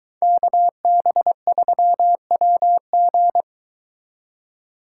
Morse code, heard as a keyed tone, about 22 wpm, 700 Hz. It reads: KB3WG